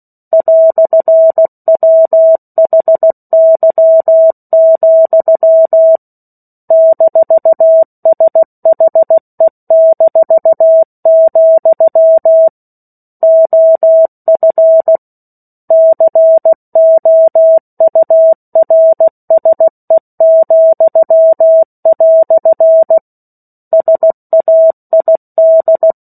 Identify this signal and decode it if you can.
Morse code, heard as a keyed tone, about 16 wpm, 650 Hz. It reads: "WHY, -SHE-, OF COURSE," SAID
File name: 03-Chapter-03 16 WPM